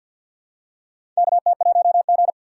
Morse code, heard as a keyed tone, about 52 wpm, 700 Hz. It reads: KT1G
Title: USA Call Signs - 4 Characters - 3x Character Spacing - 50wpm